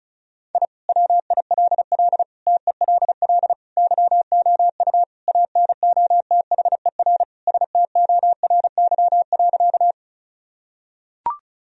Morse code, heard as a keyed tone, about 34 wpm, 700 Hz. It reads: I WILL TELL YOU ANOTHER STORY.
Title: Sentences from Top 1000 Words - Review Part 1 - 35wpm